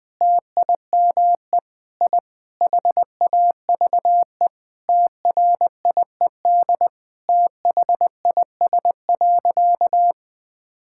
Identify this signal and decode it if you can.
Morse code, heard as a keyed tone, about 20 wpm, 700 Hz. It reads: TIME I HAVE TRIED THIS.
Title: Sentences with 4-Word Phrases - 20wpm